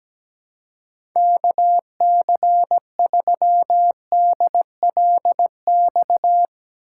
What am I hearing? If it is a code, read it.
Morse code, heard as a keyed tone, about 17 wpm, 700 Hz. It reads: KC3DLX
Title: USA Call Signs - 6 Characters - 2x Character Spacing - 17wpm